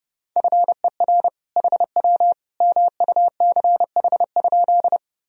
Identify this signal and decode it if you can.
Morse code, heard as a keyed tone, about 30 wpm, 700 Hz. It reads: FER HW MUCH?
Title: Sentences from Top 200 Words and Abbreviations 30wpm